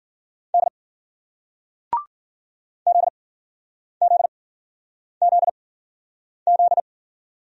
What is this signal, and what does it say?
Morse code, heard as a keyed tone, about 54 wpm, spaced as for 9 wpm, 700 Hz. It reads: D ZZZZ